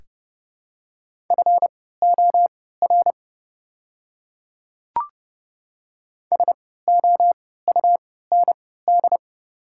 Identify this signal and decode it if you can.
Morse code, heard as a keyed tone, about 30 wpm, spaced as for 17 wpm, 700 Hz. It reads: FOR SOUND
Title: 3x Character Spacing - Top 300 Words - 30wpm